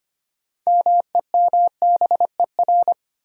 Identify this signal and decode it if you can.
Morse code, heard as a keyed tone, about 25 wpm, 700 Hz. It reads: MEMBER
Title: Sets of 3 Words - New Word Drill - 25wpm